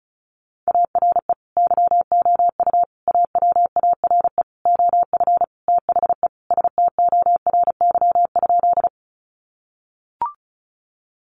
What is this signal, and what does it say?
Morse code, heard as a keyed tone, about 34 wpm, 700 Hz. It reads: ARE YOU AWARE OF THE STORY?
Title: Sentences from Top 1200 Words 35wpm